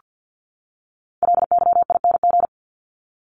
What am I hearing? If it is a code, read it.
Morse code, heard as a keyed tone, about 49 wpm, 700 Hz. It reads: LYING
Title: ING Suffix - Top 100 Words with Base Words 50wpm